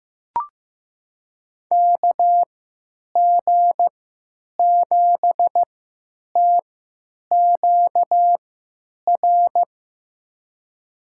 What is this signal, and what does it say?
Morse code, heard as a keyed tone, about 15 wpm, spaced as for 9 wpm, 700 Hz. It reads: KG7TQR
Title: USA Call Signs - 6 Characters - 3x Character Spacing - 15wpm